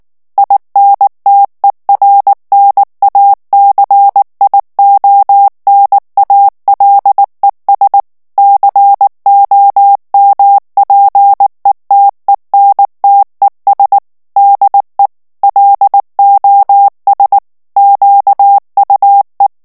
Morse code, heard as a keyed tone, about 19 wpm, 800 Hz. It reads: INTERNACIONALES COMPETENTES DE LOS QUE
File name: morse mp3 / ESPAGNOL TEXTE EN CLAIR 2 VITESSE 18 MOTS MINUTES